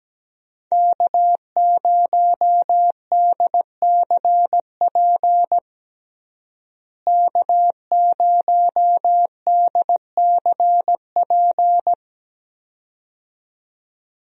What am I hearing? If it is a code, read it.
Morse code, heard as a keyed tone, about 17 wpm, 700 Hz. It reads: K0DCP K0DCP
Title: USA Call Signs - 5 Characters - Repeated 2x - 17wpm